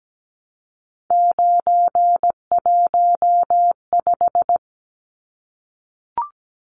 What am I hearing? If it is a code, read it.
Morse code, heard as a keyed tone, about 17 wpm, 700 Hz. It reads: 915
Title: Numbers - 3-Digit 17wpm